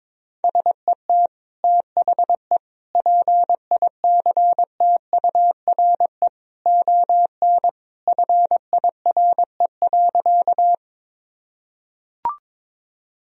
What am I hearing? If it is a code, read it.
Morse code, heard as a keyed tone, about 21 wpm, 700 Hz. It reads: SET THE PICTURE ON FIRE.
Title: Sentences from Top 1000 Words - Review Part 1 - 22wpm